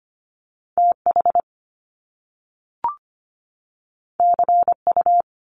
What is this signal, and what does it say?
Morse code, heard as a keyed tone, about 25 wpm, 700 Hz. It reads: TH CU
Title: Top Two Letter Combinations in Words 25wpm